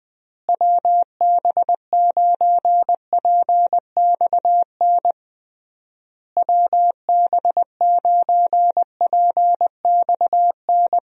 Morse code, heard as a keyed tone, about 21 wpm, 700 Hz. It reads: WB9PXN WB9PXN
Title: USA Call Signs - 6 Characters - Repeated 3x - 20wpm